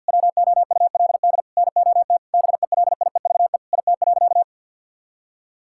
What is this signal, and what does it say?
Morse code, heard as a keyed tone, about 50 wpm, 700 Hz. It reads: WOULD NOT BELIEVE IT.
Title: Sentences from Top 1000 Words - Review Part 6 - 50wpm